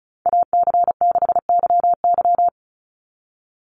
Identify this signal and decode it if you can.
Morse code, heard as a keyed tone, about 34 wpm, 700 Hz. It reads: AC6YY
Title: Call Signs Repeated 2x - 35wpm